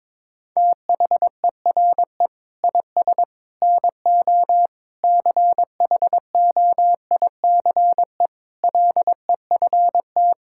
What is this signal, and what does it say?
Morse code, heard as a keyed tone, about 22 wpm, 700 Hz. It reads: THERE IS NO CHOICE LEFT
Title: Sentences from Top 1000 Words - Review Part 6 - 22wpm